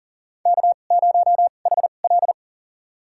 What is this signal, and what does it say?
Morse code, heard as a keyed tone, about 40 wpm, spaced as for 29 wpm, 700 Hz. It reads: K0HL
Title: USA Call Signs - 4 Characters - 2x Character Spacing - 40wpm